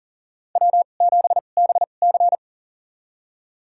Morse code, heard as a keyed tone, about 39 wpm, spaced as for 29 wpm, 700 Hz. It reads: W7BC
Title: USA Call Signs - 4 Characters - 2x Character Spacing - 40wpm